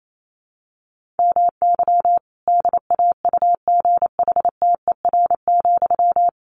Morse code, heard as a keyed tone, about 28 wpm, 700 Hz. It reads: MY DAUGHTER,
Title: Sentences from Top 1500 Words 28wpm